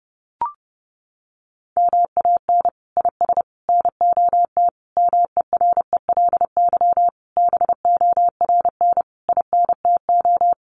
Morse code, heard as a keyed tone, about 30 wpm, 700 Hz. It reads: MAN IS NOT MERELY BORN INTO
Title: Sentences from Top 1400 Words 30wpm